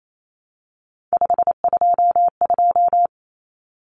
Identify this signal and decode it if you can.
Morse code, heard as a keyed tone, about 28 wpm, 700 Hz. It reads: 522
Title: Common Cut Numbers - 3-Digit 28wpm